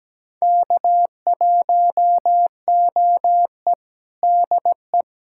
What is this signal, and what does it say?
Morse code, heard as a keyed tone, about 17 wpm, 700 Hz. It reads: K1OE DE
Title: USA Callsign DE Callsign - 4 Characters - 17wpm